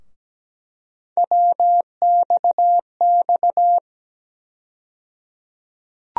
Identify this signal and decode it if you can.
Morse code, heard as a keyed tone, about 17 wpm, 700 Hz. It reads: WXX
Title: USA Call Sign Postfixes - 3 Letters 17wpm